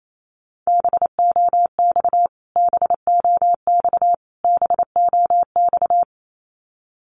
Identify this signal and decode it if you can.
Morse code, heard as a keyed tone, about 28 wpm, 700 Hz. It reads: BOX BOX BOX
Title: Top 500 Words Repeated 28wpm